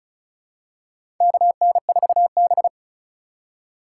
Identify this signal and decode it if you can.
Morse code, heard as a keyed tone, about 36 wpm, 700 Hz. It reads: KN4B
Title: USA Call Signs - 3x Character Spacing - 35wpm